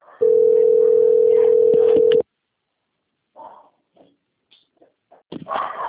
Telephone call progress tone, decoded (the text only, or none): ringback tone